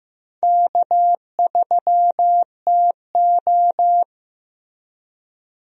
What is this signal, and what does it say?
Morse code, heard as a keyed tone, about 15 wpm, 700 Hz. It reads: K3TO
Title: USA Call Signs - 4 Characters - Repeated 3x - 15wpm